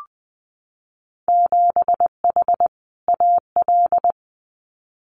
Morse code, heard as a keyed tone, about 20 wpm, 700 Hz. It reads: 7H AL
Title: Winter Field Day Exchanges Encore 20wpm